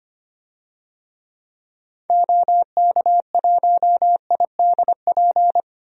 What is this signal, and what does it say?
Morse code, heard as a keyed tone, about 25 wpm, 700 Hz. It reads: OK1IDP
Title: DXCC-100 Call Signs - 6 Characters - 25wpm